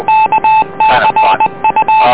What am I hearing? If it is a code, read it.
Morse code, heard as a keyed tone, about 20 wpm, 870 Hz, with noise. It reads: KCU